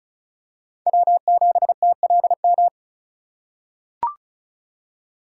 Morse code, heard as a keyed tone, about 36 wpm, 700 Hz. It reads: W7TLM
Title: USA Call Signs - 5 Characters - 3x Character Spacing - 35wpm